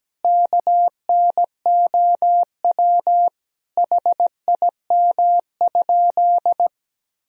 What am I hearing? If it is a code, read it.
Morse code, heard as a keyed tone, about 17 wpm, 700 Hz. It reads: KNOW HIM?
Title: Sentences from Top 100 Words and Abbreviations 17wpm